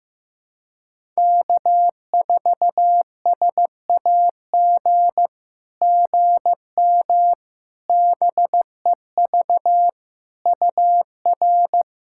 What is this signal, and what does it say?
Morse code, heard as a keyed tone, about 15 wpm, 700 Hz. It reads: K4SAG GM BEV UR
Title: SOTA QSO Practice 15wpm